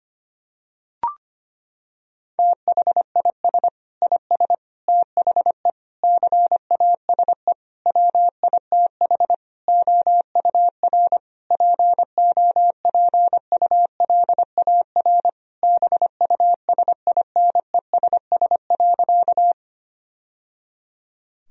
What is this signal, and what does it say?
Morse code, heard as a keyed tone, about 25 wpm, 700 Hz. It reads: THIS IS THE CASE WITH OUR POPULAR BUSINESS.
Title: Sentences from Top 1000 Words - Review Part 3 - 25wpm